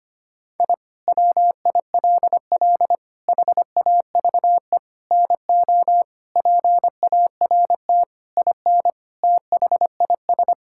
Morse code, heard as a keyed tone, about 25 wpm, 700 Hz. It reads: I WILL HAVE NO PART IN THIS
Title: Sentences from Top 1000 Words - Review Part 5 - 25wpm